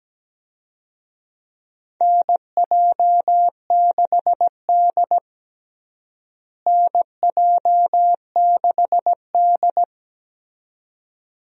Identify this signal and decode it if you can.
Morse code, heard as a keyed tone, about 17 wpm, 700 Hz. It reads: NJ6D NJ6D
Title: USA Call Signs - 4 Characters - Repeated 2x - 17wpm